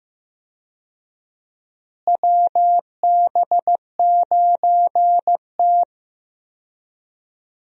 Morse code, heard as a keyed tone, about 15 wpm, 700 Hz. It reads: WB9T
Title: USA Call Signs - 4 Characters - Repeated 2x - 15wpm